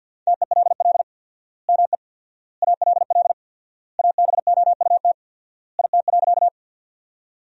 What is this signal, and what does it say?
Morse code, heard as a keyed tone, about 46 wpm, 700 Hz. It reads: TELL ME ALL ABOUT IT.
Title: 4x Word-Spacing - Sentences from Top 200 Words - 50wpm